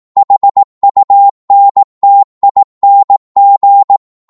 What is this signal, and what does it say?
Morse code, heard as a keyed tone, about 18 wpm, 800 Hz. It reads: HUNTING